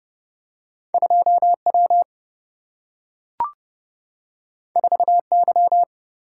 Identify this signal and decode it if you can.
Morse code, heard as a keyed tone, about 30 wpm, 700 Hz. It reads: 2W 4Y
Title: Single-Letter Single-Number 30wpm